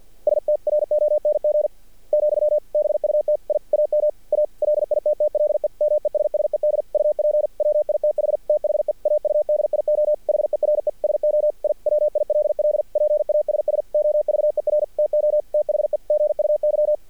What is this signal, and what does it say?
Morse code, heard as a keyed tone, about 48 wpm, 600 Hz. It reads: STRONG , BUT I AM A LITTLE MESSED UP WITH THE AUDIO HERE SO I WILL PASS OVER TO THE GUY
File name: archived / CRSnet